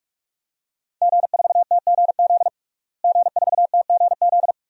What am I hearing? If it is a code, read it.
Morse code, heard as a keyed tone, about 43 wpm, 700 Hz. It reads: G4TGZ G4TGZ
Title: DXCC-100 Call Signs - 5 Characters - Repeated 2x - 45wpm